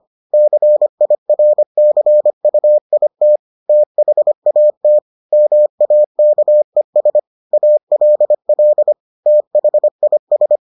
Morse code, heard as a keyed tone, about 25 wpm, 600 Hz. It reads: CIRCUIT THAT MAKES ALL THIS